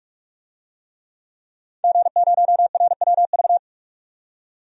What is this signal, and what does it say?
Morse code, heard as a keyed tone, about 46 wpm, 700 Hz. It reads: G0RWV
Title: DXCC-100 Call Signs - 5 Characters - 45wpm